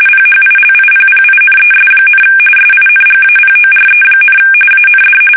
RTTY, about 45.45 baud. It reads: PHILIPPINES, SINGAPORE, AND
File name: fsk850 / battleshipnj